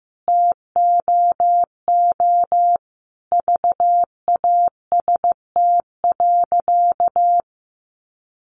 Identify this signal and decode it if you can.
Morse code, heard as a keyed tone, about 15 wpm, 700 Hz. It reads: TOO VAST.
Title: Sentences from Top 1300 Words 15wpm